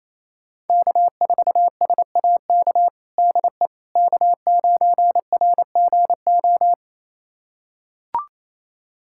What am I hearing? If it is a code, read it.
Morse code, heard as a keyed tone, about 28 wpm, 690 Hz. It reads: K4SAK DE K9RGO
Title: USA Callsign DE Callsign - 5 Characters - 28wpm